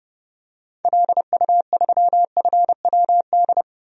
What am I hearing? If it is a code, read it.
Morse code, heard as a keyed tone, about 30 wpm, 700 Hz. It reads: LU3FWD
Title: DXCC-100 Call Signs - 6 Characters - 30wpm